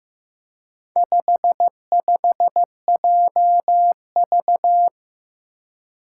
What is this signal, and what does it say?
Morse code, heard as a keyed tone, about 15 wpm, 700 Hz. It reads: 55JV
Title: Four Challenging Letter-Number Combinations 15wpm